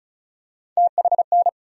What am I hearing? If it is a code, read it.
Morse code, heard as a keyed tone, about 34 wpm, 700 Hz. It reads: THN
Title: USA Call Sign Postfixes - 3 Letters 35wpm